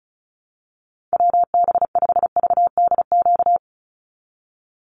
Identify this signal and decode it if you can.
Morse code, heard as a keyed tone, about 34 wpm, 700 Hz. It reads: WB5VDQ
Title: USA Call Signs 35wpm